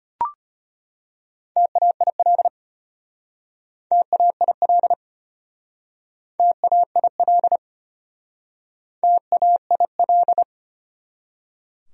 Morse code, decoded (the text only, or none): TAIL TAIL TAIL TAIL